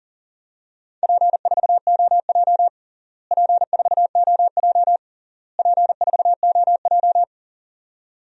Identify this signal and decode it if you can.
Morse code, heard as a keyed tone, about 40 wpm, 700 Hz. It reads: P4OJ P4OJ P4OJ
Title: DXCC-100 Call Signs - 4 Characters - Repeated 3x - 40wpm